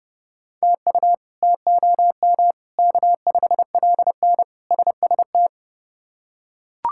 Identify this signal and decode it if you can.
Morse code, heard as a keyed tone, about 30 wpm, 700 Hz. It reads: TU TOM K5LN SST
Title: K1USN SST QSO Practice - 30wpm